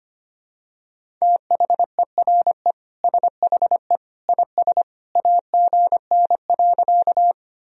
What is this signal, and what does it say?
Morse code, heard as a keyed tone, about 25 wpm, 700 Hz. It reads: THERE SHE IS AGN.
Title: Sentences from Top 200 Words and Abbreviations 25wpm